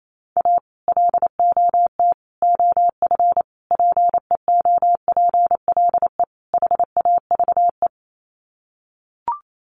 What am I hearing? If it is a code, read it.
Morse code, heard as a keyed tone, about 28 wpm, 700 Hz. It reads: A LOT OF PEOPLE HAVE
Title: Sets of 5 Words 28wpm